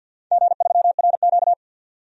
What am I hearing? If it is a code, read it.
Morse code, heard as a keyed tone, about 51 wpm, 700 Hz. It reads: G3RQ
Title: DXCC-100 Call Signs - 4 Characters - 50wpm